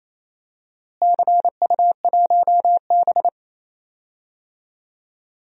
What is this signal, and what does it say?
Morse code, heard as a keyed tone, about 28 wpm, 700 Hz. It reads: CU1B
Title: DXCC-100 Call Signs - 4 Characters - Repeated 3x - 28wpm